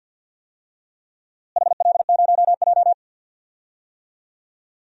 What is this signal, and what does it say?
Morse code, heard as a keyed tone, about 52 wpm, 700 Hz. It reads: HL0J
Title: DXCC-100 Call Signs - 4 Characters - Repeated 3x - 50wpm